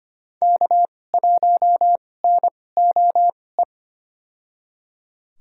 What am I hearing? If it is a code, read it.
Morse code, heard as a keyed tone, about 25 wpm, spaced as for 18 wpm, 700 Hz. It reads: K1NOE
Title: USA Call Signs - 2x Character Spacing - 25wpm